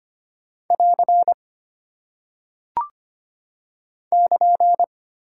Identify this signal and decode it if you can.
Morse code, heard as a keyed tone, about 25 wpm, 700 Hz. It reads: <AR> <KN>
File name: practice / 25wpm / Single Punctuation-Prosign - 25wpm